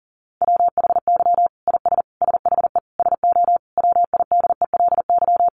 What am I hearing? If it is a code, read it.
Morse code, heard as a keyed tone, about 40 wpm, 700 Hz. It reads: WHY IS SHE SO WIDELY
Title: Sentences from Top 1200 Words 40wpm